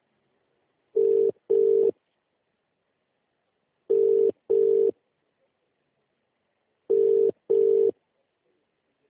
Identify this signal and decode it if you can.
Ringback tone (UK), 400 + 450 Hz.